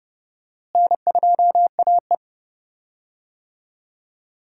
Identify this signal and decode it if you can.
Morse code, heard as a keyed tone, about 30 wpm, 700 Hz. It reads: N2AE
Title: USA Call Signs - 4 Characters - Repeated 3x - 30wpm